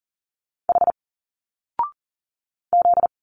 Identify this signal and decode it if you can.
Morse code, heard as a keyed tone, about 40 wpm, 700 Hz. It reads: H Z
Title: Single Letters 40wpm